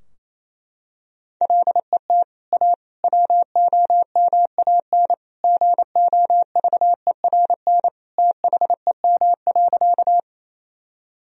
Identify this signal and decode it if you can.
Morse code, heard as a keyed tone, about 28 wpm, 700 Hz. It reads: LET A WOMAN GOVERN THEM.
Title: Sentences from Top 1000 Words - Review Part 2 - 28wpm